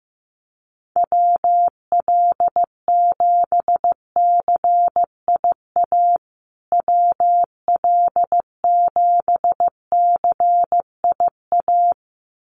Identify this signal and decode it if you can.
Morse code, heard as a keyed tone, about 15 wpm, 700 Hz. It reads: WL7CIA WL7CIA
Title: USA Call Signs Repeated 2x 15wpm